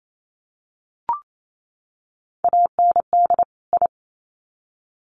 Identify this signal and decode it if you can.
Morse code, heard as a keyed tone, about 28 wpm, 700 Hz. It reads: AND I
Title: Sets of 2 Words 28wpm